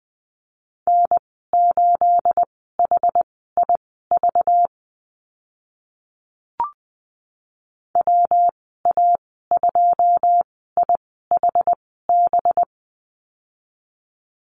Morse code, heard as a keyed tone, about 20 wpm, spaced as for 14 wpm, 700 Hz. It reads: N8HIV WA2IHB
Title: 238 Lesson 38g - Letter q - Call Signs 20wpm